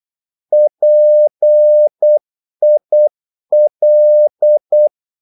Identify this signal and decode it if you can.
Morse code, heard as a keyed tone, about 8 wpm, 600 Hz. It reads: PIL
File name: Korte ord DK 8wpm